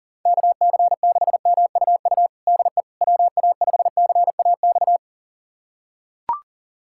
Morse code, heard as a keyed tone, about 37 wpm, 700 Hz. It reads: KC6MUU DE WA5CAX
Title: USA Callsign DE Callsign - 6 Characters - 40wpm